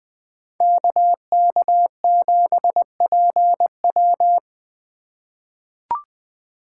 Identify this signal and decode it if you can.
Morse code, heard as a keyed tone, about 20 wpm, 700 Hz. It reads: KK7PW
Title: USA Call Signs - 5 Characters - 4x Character Spacing - 20wpm